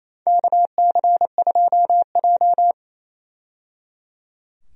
Morse code, heard as a keyed tone, about 28 wpm, 700 Hz. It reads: KC2J
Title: USA Call Signs - 4 Characters - 28wpm